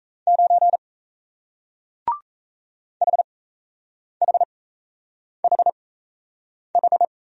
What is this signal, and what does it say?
Morse code, heard as a keyed tone, about 37 wpm, spaced as for 8 wpm, 700 Hz. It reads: 9 HHHH